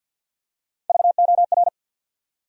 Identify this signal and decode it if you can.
Morse code, heard as a keyed tone, about 49 wpm, 700 Hz. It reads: VOR
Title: USA Call Sign Postfixes - 3 Letters 50wpm